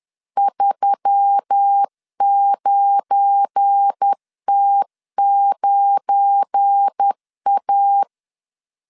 Morse code, heard as a keyed tone, about 10 wpm, 800 Hz. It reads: 39T9A
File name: LEZIONI / 7 GF9Z / 7 GF9 10wpm 2747